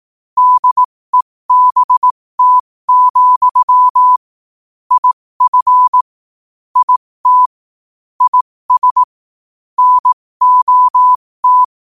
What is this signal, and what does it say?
Morse code, heard as a keyed tone, about 18 wpm, spaced as for 15 wpm, 1000 Hz. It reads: DEBT, IF IT IS NOT
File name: qotdfiles / 15wpm